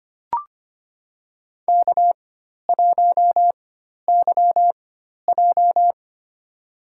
Morse code, heard as a keyed tone, about 25 wpm, spaced as for 12 wpm, 700 Hz. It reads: K1YJ
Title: USA Call Signs - 4 Characters - 4x Character Spacing - 25wpm